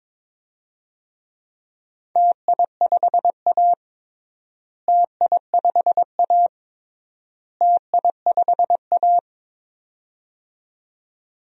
Morse code, heard as a keyed tone, about 21 wpm, 700 Hz. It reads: TI5A TI5A TI5A
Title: DXCC-100 Call Signs - 4 Characters - Repeated 3x - 22wpm